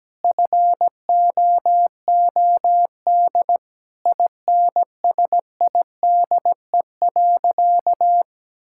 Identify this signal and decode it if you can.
Morse code, heard as a keyed tone, about 17 wpm, 700 Hz. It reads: FOOD INSIDE.
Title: Sentences from Top 1000 Words - Review Part 6 - 17wpm